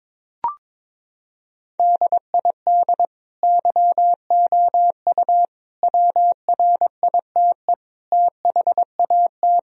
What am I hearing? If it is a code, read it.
Morse code, heard as a keyed tone, about 21 wpm, 700 Hz. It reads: DID YOU WRITE THAT
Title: Sentences from Top 1000 Words - Review Part 3 - 22wpm